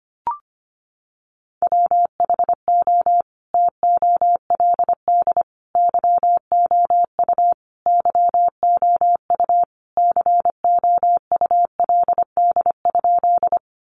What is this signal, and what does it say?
Morse code, heard as a keyed tone, about 25 wpm, 700 Hz. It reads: WHO TOLD YOU YOU COULD?
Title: Sentences from Top 400 Words 25wpm